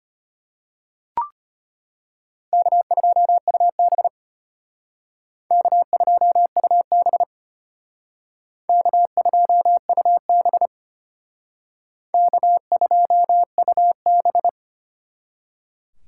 Morse code, heard as a keyed tone, about 32 wpm, 700 Hz. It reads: K2UB K2UB K2UB K2UB